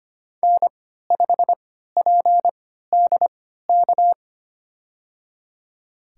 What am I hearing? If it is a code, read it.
Morse code, heard as a keyed tone, about 25 wpm, spaced as for 14 wpm, 700 Hz. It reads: N5PDK
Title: USA Call Signs - 3x Character Spacing - 25wpm